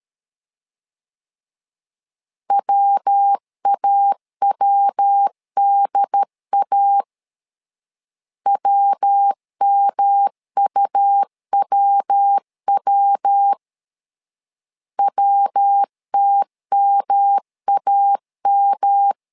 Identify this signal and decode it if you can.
Morse code, heard as a keyed tone, about 13 wpm, 800 Hz. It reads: WAWDA WMUWW WTMAM
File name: LEZIONI / 3 MWRO / 3 MW 12wpm 1633